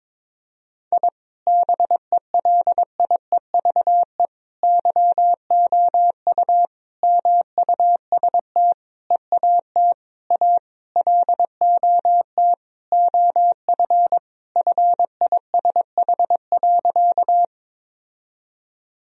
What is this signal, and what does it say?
Morse code, heard as a keyed tone, about 22 wpm, 700 Hz. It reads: I BELIEVE YOU MUST EAT A LOT OF FISH.
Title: Sentences from Top 1000 Words - Review Part 3 - 22wpm